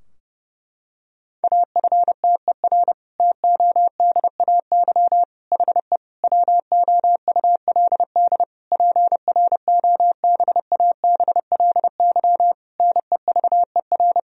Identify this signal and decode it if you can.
Morse code, heard as a keyed tone, about 30 wpm, 700 Hz. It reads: AFTER TODAY HE WOULD PROBABLY NEVER
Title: Sentences from Top 1000 Words - Review Part 4 - 30wpm